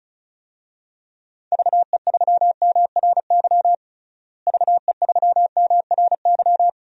Morse code, heard as a keyed tone, about 34 wpm, 700 Hz. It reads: VE3MRY VE3MRY
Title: DXCC-100 Call Signs - 6 Characters - Repeated 3x - 35wpm